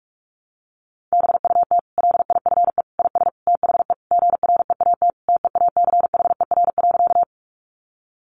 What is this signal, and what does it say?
Morse code, heard as a keyed tone, about 44 wpm, 700 Hz. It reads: BUT LIFE IS THE GREAT TEACHER.
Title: Sentences from Top 900 Words 45wpm